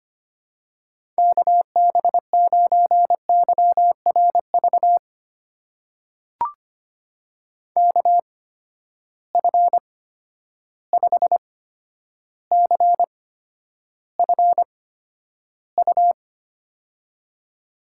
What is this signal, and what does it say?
Morse code, heard as a keyed tone, about 25 wpm, spaced as for 11 wpm, 700 Hz. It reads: KB9YRV KF5CFU
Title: USA Call Signs - 6 Characters - 8x Character Spacing - 25wpm